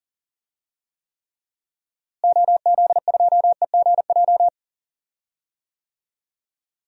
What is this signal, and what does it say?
Morse code, heard as a keyed tone, about 39 wpm, 700 Hz. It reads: OZ2EGJ
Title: DXCC-100 Call Signs - 6 Characters - Repeated 2x - 40wpm